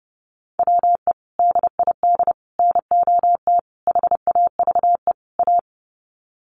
Morse code, heard as a keyed tone, about 30 wpm, 700 Hz. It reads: WE DID NOT HAVE A
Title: Sets of 5 Words Encore2 30wpm